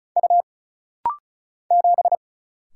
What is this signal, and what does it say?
Morse code, heard as a keyed tone, about 36 wpm, 700 Hz. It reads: U 7
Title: Single Letter-Number Alt - Rapid-Fire 35wpm